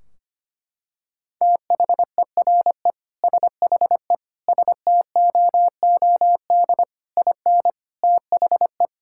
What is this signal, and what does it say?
Morse code, heard as a keyed tone, about 25 wpm, 700 Hz. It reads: THERE SHE STOOD IN THE
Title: Sentences from Top 1600 Words 25wpm